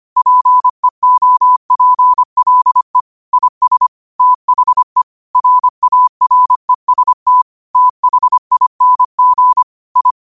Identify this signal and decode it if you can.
Morse code, heard as a keyed tone, about 25 wpm, 1000 Hz. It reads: PEOPLE IS THE RAREST THING I
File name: qotdfiles / 25wpm